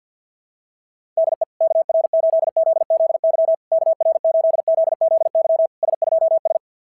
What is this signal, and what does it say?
Morse code, heard as a keyed tone, about 52 wpm, 690 Hz. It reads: DE KR8ZZY KR8ZZY S2S
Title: SOTA QSO Practice 50wpm